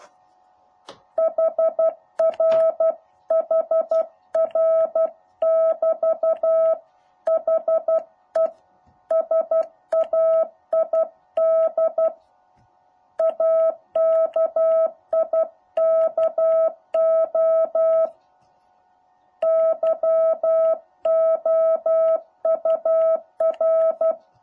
Morse code, heard as a keyed tone, about 12 wpm, 650 Hz. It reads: HR HR<BT>HE SAID AKIKO YOUR
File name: jh8cbh / 2021cw